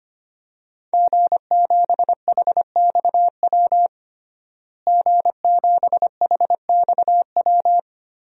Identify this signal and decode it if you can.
Morse code, heard as a keyed tone, about 25 wpm, 700 Hz. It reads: G7HXW G7HXW
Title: DXCC-100 Call Signs - 5 Characters - Repeated 2x - 25wpm